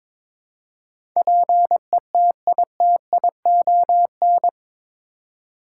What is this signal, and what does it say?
Morse code, heard as a keyed tone, about 22 wpm, 700 Hz. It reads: PETITION
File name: practice / 22wpm / Eight Letter Words - 22wpm